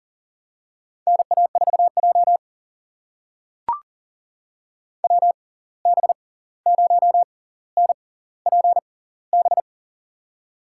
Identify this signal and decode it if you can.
Morse code, heard as a keyed tone, about 40 wpm, spaced as for 18 wpm, 700 Hz. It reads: NA4J WB0NPB
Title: USA Call Signs - 6x Character Spacing - 40wpm